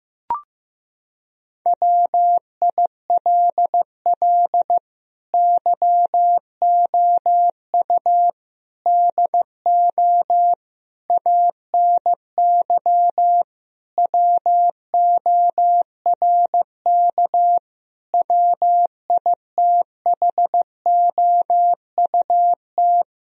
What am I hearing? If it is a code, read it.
Morse code, heard as a keyed tone, about 15 wpm, 700 Hz. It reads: WILL YOU DO ANY WORK WITHOUT
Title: Sentences from Top 1000 Words - Review Part 3 - 15wpm